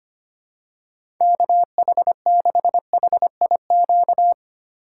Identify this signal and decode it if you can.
Morse code, heard as a keyed tone, about 25 wpm, 700 Hz. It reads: KH6HIQ
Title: USA Call Signs - 6 Characters - 2x Character Spacing - 25wpm